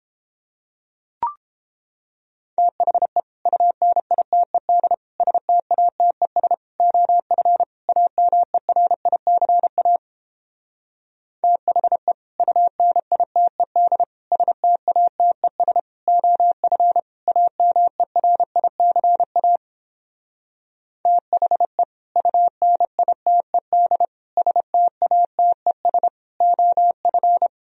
Morse code, heard as a keyed tone, about 29 wpm, 700 Hz. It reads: THE UNITED STATES OF AMERICA THE UNITED STATES OF AMERICA THE UNITED STATES OF